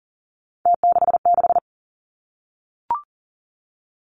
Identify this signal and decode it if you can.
Morse code, heard as a keyed tone, about 39 wpm, 700 Hz. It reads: T66
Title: Three Challenging Letter-Number Combinations 40wpm